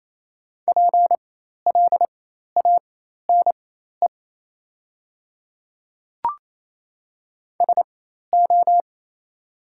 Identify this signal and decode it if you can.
Morse code, heard as a keyed tone, about 28 wpm, spaced as for 13 wpm, 700 Hz. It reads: PLANE SO
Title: 4x Character Spacing - Top 500 Words - 28wpm